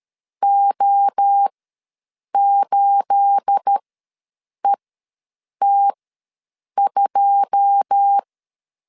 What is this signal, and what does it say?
Morse code, heard as a keyed tone, about 13 wpm, spaced as for 7 wpm, 800 Hz. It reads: O8ET2
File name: LEZIONI / 9 28 ws 12wpm 3354